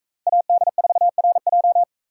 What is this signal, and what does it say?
Morse code, heard as a keyed tone, about 43 wpm, 700 Hz. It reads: AD4RJ